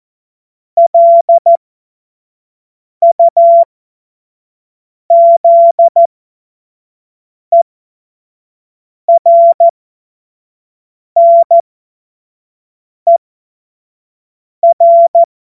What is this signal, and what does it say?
Morse code, heard as a keyed tone, about 14 wpm, spaced as for 5 wpm, 680 Hz. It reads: LUZERNER
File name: Morsekurs / Witze / Witz 02. Februar 2021, 25 F70